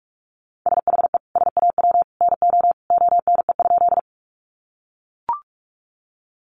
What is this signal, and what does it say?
Morse code, heard as a keyed tone, about 44 wpm, 700 Hz. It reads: SHE SAW NO ONE?
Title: Sentences from Top 300 Words 45wpm